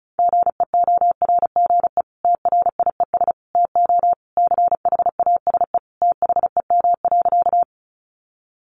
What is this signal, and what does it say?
Morse code, heard as a keyed tone, about 34 wpm, 700 Hz. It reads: GEORGE TRIES TO CHASE THEM.
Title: Additional Easy Phrases 35wpm